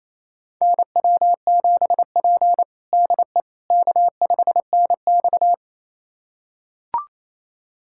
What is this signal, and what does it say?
Morse code, heard as a keyed tone, about 28 wpm, 700 Hz. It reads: NW7P DE K5NX
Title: USA Callsign DE Callsign - 4 Characters - 28wpm